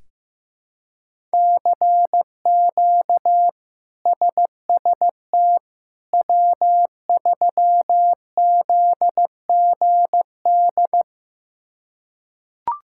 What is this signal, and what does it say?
Morse code, heard as a keyed tone, about 15 wpm, 700 Hz. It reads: CQ SST W3ZGD
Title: K1USN SST QSO Practice - 15wpm